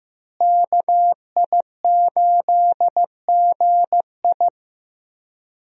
Morse code, heard as a keyed tone, about 16 wpm, 700 Hz. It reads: KI8GI
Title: USA Call Signs - 5 Characters - Repeated 2x - 15wpm